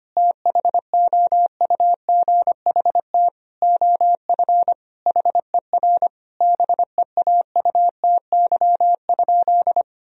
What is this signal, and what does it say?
Morse code, heard as a keyed tone, about 25 wpm, 700 Hz. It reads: THOUGHT OF HER BEAUTY?
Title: Sentences from Top 1000 Words - Review Part 2 - 25wpm